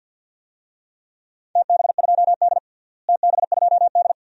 Morse code, heard as a keyed tone, about 45 wpm, 700 Hz. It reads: TB2D TB2D
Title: DXCC-100 Call Signs - 4 Characters - Repeated 2x - 50wpm